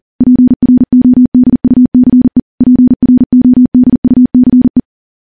Morse code, heard as a keyed tone, about 40 wpm, 250 Hz. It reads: PRODUCE PRODUCE
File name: morse / morse-training-40WPM